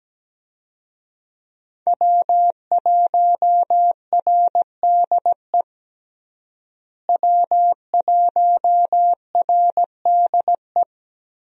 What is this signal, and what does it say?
Morse code, heard as a keyed tone, about 17 wpm, 700 Hz. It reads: W1RDE W1RDE
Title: USA Call Signs - 5 Characters - Repeated 2x - 17wpm